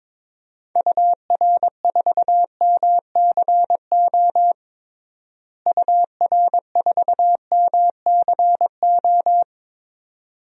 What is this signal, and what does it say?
Morse code, heard as a keyed tone, about 22 wpm, 700 Hz. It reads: UR4MCO UR4MCO
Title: DXCC-100 Call Signs - 6 Characters - Repeated 2x - 22wpm